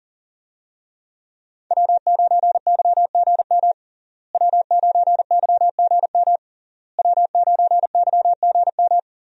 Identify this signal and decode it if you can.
Morse code, heard as a keyed tone, about 37 wpm, 700 Hz. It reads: W9YGM W9YGM W9YGM
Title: USA Call Signs - 5 Characters - Repeated 3x - 40wpm